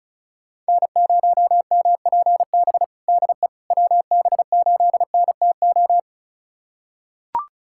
Morse code, heard as a keyed tone, about 36 wpm, 700 Hz. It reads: N0MPB DE WB8NTO
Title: USA Callsign DE Callsign 35wpm